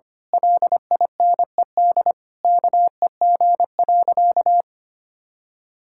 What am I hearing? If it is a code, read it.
Morse code, heard as a keyed tone, about 25 wpm, 700 Hz. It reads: LINED KEG.
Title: Pangram Challenge v2 - 25wpm